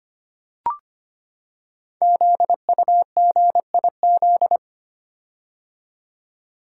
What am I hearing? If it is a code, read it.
Morse code, heard as a keyed tone, about 25 wpm, 700 Hz. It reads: ZUGIZ
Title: Code Groups - 5 Letter:Numbers - 25wpm